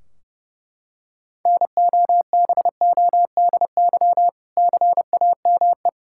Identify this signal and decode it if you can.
Morse code, heard as a keyed tone, about 30 wpm, 700 Hz. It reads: NOBODY CAME
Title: Sentences from Top 1600 Words 30wpm